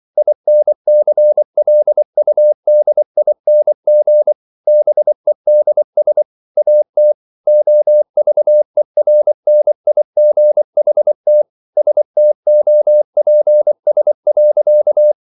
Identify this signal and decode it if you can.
Morse code, heard as a keyed tone, about 24 wpm, 600 Hz. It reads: INCLUDING BEDS AT OVERNIGHT STOPS.